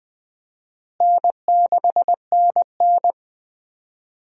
Morse code, heard as a keyed tone, about 21 wpm, 700 Hz. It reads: N6NN
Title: USA Call Signs - 4 Characters - Repeated 2x - 20wpm